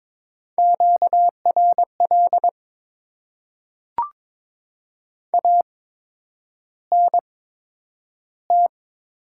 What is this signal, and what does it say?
Morse code, heard as a keyed tone, about 22 wpm, spaced as for 9 wpm, 700 Hz. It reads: QRL ANT
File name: Top 20 QSO Elements - 8x Character Spacing - 22wpm